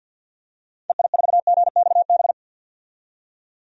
Morse code, heard as a keyed tone, about 50 wpm, 700 Hz. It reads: EI4GXB
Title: DXCC-100 Call Signs - 6 Characters - 50wpm